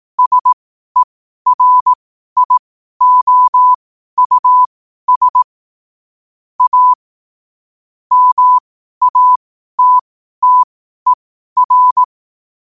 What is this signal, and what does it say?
Morse code, heard as a keyed tone, about 18 wpm, spaced as for 12 wpm, 1000 Hz. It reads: SERIOUS A MATTER
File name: qotdfiles / 12wpm